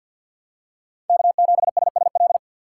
Morse code, heard as a keyed tone, about 49 wpm, 700 Hz. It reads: K7SSL
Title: USA Call Signs - 5 Characters - 3x Character Spacing - 50wpm